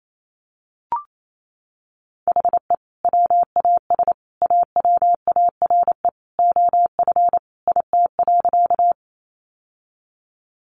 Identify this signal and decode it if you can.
Morse code, heard as a keyed tone, about 28 wpm, 700 Hz. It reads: HE WAS AWARE OF IT.
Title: Sentences from Top 1200 Words 28wpm